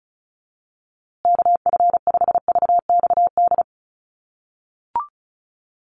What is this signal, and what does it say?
Morse code, heard as a keyed tone, about 34 wpm, 700 Hz. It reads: KF5VXD
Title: USA Call Signs Repeated 3x 35wpm